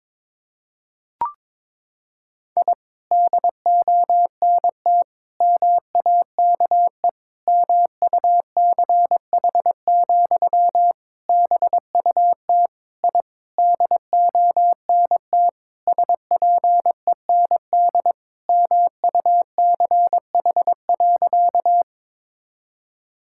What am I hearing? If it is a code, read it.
Morse code, heard as a keyed tone, about 22 wpm, 700 Hz. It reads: I DONT MAKE MUCH, BUT I DONT SPEND MUCH.
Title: Sentences from Top 1000 Words - Review Part 5 - 22wpm